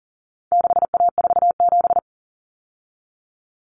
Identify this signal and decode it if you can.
Morse code, heard as a keyed tone, about 40 wpm, 700 Hz. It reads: 6A47
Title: Common Cut Numbers - 4-Digit 40wpm